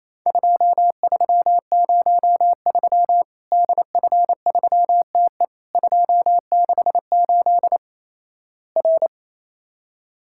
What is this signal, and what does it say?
Morse code, heard as a keyed tone, about 28 wpm, 700 Hz. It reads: 2303 DF3TE 268 R
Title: Worked All Europe DX Contest - QTC Exchange - 28wpm